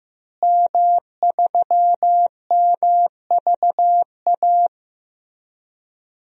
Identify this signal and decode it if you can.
Morse code, heard as a keyed tone, about 15 wpm, 700 Hz. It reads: M3MVA
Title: DXCC-100 Call Signs - 5 Characters - Repeated 2x - 15wpm